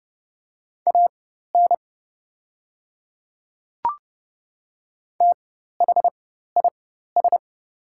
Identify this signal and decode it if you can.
Morse code, heard as a keyed tone, about 30 wpm, spaced as for 14 wpm, 700 Hz. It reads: AN THIS
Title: 4x Character Spacing - Top 100 Words - 30wpm